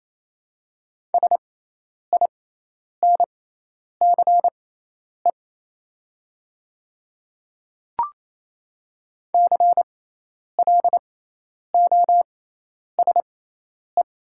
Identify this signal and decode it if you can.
Morse code, heard as a keyed tone, about 27 wpm, spaced as for 10 wpm, 700 Hz. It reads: SINCE CLOSE
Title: 6x Character Spacing - Top 300 Words - 28wpm